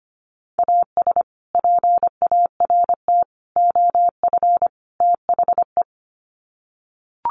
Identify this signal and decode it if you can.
Morse code, heard as a keyed tone, about 25 wpm, 700 Hz. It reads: AS PART OF THE
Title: Sets of 4 Words 25wpm